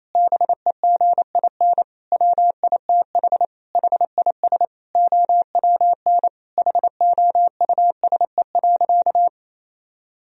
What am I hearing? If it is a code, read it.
Morse code, heard as a keyed tone, about 28 wpm, 700 Hz. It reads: BEGIN WITH HIS OWN HOUSE.
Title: Sentences from Top 1000 Words - Review Part 1 - 28wpm